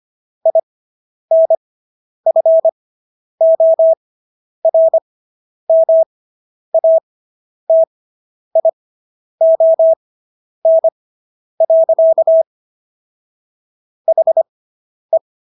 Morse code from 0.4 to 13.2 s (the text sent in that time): INFORMATION.